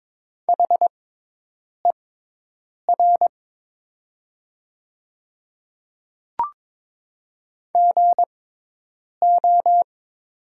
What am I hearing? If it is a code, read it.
Morse code, heard as a keyed tone, about 22 wpm, spaced as for 8 wpm, 700 Hz. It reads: HER GO
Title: 6x Character Spacing - Top 100 Words - 22wpm